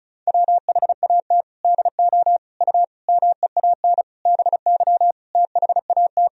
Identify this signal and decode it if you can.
Morse code, heard as a keyed tone, about 36 wpm, 700 Hz. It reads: WHAT DO U MEAN BY THAT
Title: Sentences from Top 400 Words and Abbreviations 35wpm